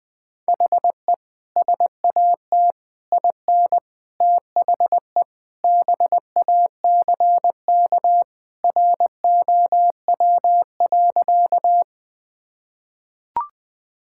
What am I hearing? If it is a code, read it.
Morse code, heard as a keyed tone, about 21 wpm, 700 Hz. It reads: HE SAT IN THE BACK ROW.
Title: Sentences with 3-Word Phrases - Spoken Beginning - 20wpm